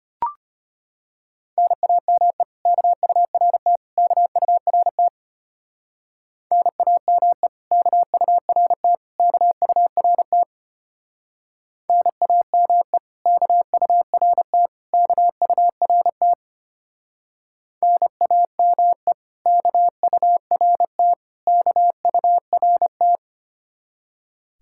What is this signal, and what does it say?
Morse code, heard as a keyed tone, about 30 wpm, 700 Hz. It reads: NAME KURT KURT NAME KURT KURT NAME KURT KURT NAME KURT KURT
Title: Speed-Racing Common QSO Phrases - 25wpm